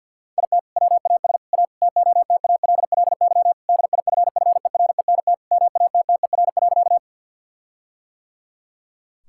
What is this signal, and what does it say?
Morse code, heard as a keyed tone, about 50 wpm, 700 Hz. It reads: IT WAS A TOTALLY DIFFERENT MATTER.
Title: Sentences from Top 1600 Words 50wpm